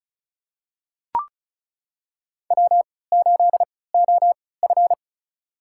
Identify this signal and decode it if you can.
Morse code, heard as a keyed tone, about 34 wpm, spaced as for 20 wpm, 700 Hz. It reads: W8OF
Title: USA Call Signs - 4 Characters - 3x Character Spacing - 35wpm